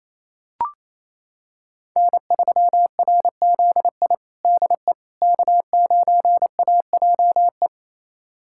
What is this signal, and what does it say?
Morse code, heard as a keyed tone, about 28 wpm, 700 Hz. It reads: N3RZI DE K9AJE
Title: USA Callsign DE Callsign - 5 Characters - 28wpm